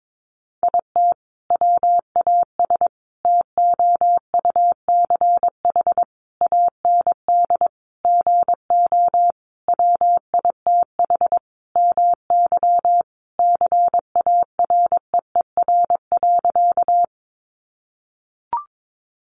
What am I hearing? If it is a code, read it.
Morse code, heard as a keyed tone, about 22 wpm, 700 Hz. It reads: IT WAS TOUCH AND GO WITH MY CAREER.
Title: Sentences from Top 1200 Words 22wpm